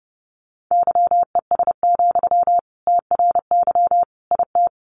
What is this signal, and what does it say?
Morse code, heard as a keyed tone, about 30 wpm, 700 Hz. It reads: YES, TRY IT
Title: Sentences from Top 600 Words 30wpm